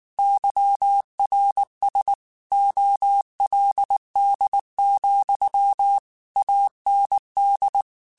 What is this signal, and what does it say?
Morse code, heard as a keyed tone, about 19 wpm, 800 Hz. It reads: YRS OLD, AND
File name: QSOs / QSOs-mp3s / kc7haa@18